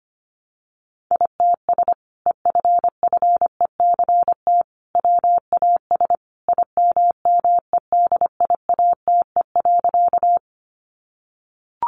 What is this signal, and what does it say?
Morse code, heard as a keyed tone, about 25 wpm, 700 Hz. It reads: ITS EFFECT WAS IMMEDIATE.
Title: Sentences from Top 1200 Words 25wpm